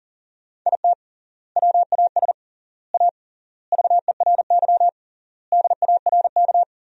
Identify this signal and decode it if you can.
Morse code, heard as a keyed tone, about 39 wpm, 700 Hz. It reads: IT WAS A VERY DARK
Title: 3x Word-Spacing - Sentences from Top 500 Words - 40wpm